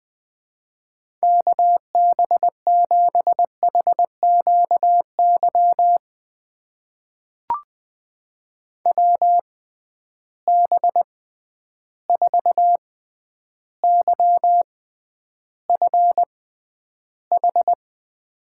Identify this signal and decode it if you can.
Morse code, heard as a keyed tone, about 20 wpm, spaced as for 10 wpm, 700 Hz. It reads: KB7HQY WB4YFH
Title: USA Call Signs - 6 Characters - 6x Character Spacing - 20wpm